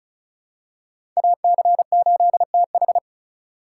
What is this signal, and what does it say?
Morse code, heard as a keyed tone, about 36 wpm, 700 Hz. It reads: AC8TH
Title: USA Call Signs - 5 Characters - 4x Character Spacing - 35wpm